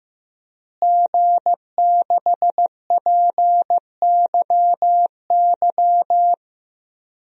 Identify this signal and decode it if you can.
Morse code, heard as a keyed tone, about 15 wpm, 700 Hz. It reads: G6PYY
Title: DXCC-100 Call Signs - 5 Characters - Repeated 3x - 15wpm